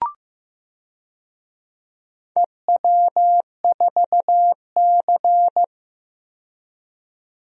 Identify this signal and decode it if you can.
Morse code, heard as a keyed tone, about 15 wpm, 700 Hz. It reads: EW4C
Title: DXCC-100 Call Signs - 4 Characters - 15wpm